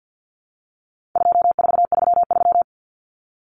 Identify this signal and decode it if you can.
Morse code, heard as a keyed tone, about 48 wpm, 700 Hz. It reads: 2433
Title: Common Cut Numbers - 4-Digit 50wpm